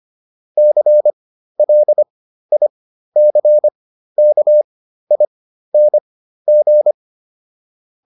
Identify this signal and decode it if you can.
Morse code, heard as a keyed tone, about 25 wpm, spaced as for 13 wpm, 600 Hz. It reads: CLICKING